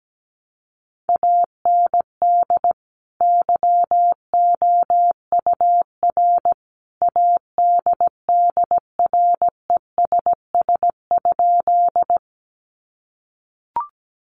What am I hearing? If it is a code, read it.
Morse code, heard as a keyed tone, about 17 wpm, 700 Hz. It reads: AND YOUR ADDRESS?
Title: Sentences from Top 1000 Words 17wpm